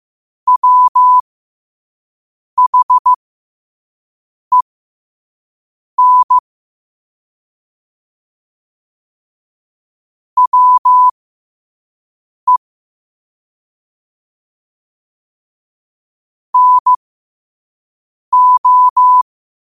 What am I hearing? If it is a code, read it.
Morse code, heard as a keyed tone, about 15 wpm, spaced as for 5 wpm, 1000 Hz. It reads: WHEN WE NO
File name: qotdfiles / 5wpm